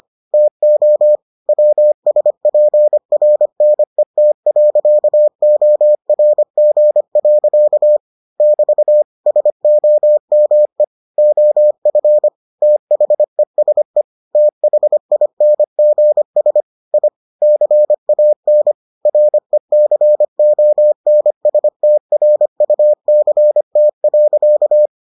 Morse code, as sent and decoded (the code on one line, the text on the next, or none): - --- / .-- ... .--. .-. -. - .-.-.- --- .-. --. .-.-.- / -...- / ... --- -- . / --- ..-. / - .... ... / - .... .. -. --. ... / .. / -.-. .- -. / .-. -.-. --- -. ... - .-. ..- -.-. - .-.-.-
TO WSPRNET.ORG. <BT> SOME OF THESE THINGS I CAN RECONSTRUCT.